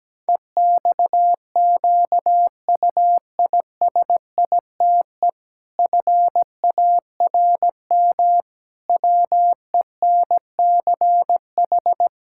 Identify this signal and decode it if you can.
Morse code, heard as a keyed tone, about 17 wpm, 700 Hz. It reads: EXQUISITE FARM WENCH
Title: Pangram Challenge v2 - 17wpm